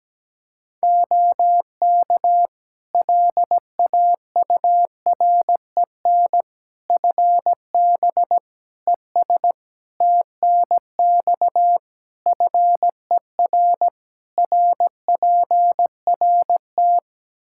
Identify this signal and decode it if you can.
Morse code, heard as a keyed tone, about 17 wpm, 700 Hz. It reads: OK LAUREN FB ES TNX FER RPRT
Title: Practice QSO - Exchange 2 - Info - 17wpm